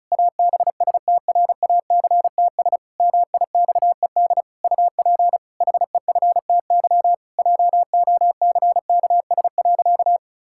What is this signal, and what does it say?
Morse code, heard as a keyed tone, about 36 wpm, 700 Hz. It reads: ABSTRACTS MIXED UP HEFTY JOCKS.
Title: Pangram Challenge v2 - 35wpm